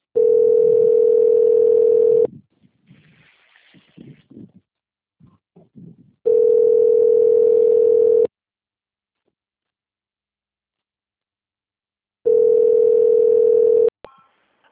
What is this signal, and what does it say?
Ringback tone (North America), 440 + 480 Hz.